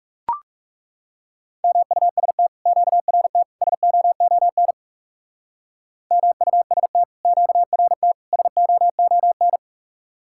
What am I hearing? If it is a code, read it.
Morse code, heard as a keyed tone, about 38 wpm, 700 Hz. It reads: MUST QRT SOON MUST QRT SOON